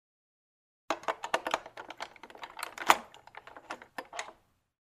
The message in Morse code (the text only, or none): none